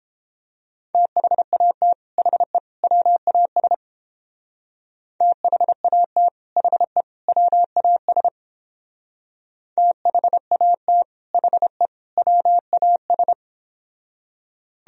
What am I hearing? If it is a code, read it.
Morse code, heard as a keyed tone, about 29 wpm, 700 Hz. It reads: THAT HE WAS THAT HE WAS THAT HE WAS